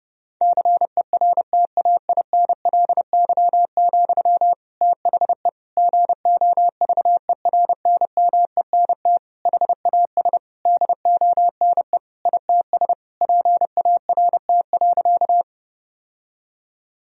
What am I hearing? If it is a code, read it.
Morse code, heard as a keyed tone, about 32 wpm, 700 Hz. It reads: CERTAINLY, THE GOVERNMENT HAS DONE ITS PART.
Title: Sentences from Top 1000 Words - Review Part 3 - 30wpm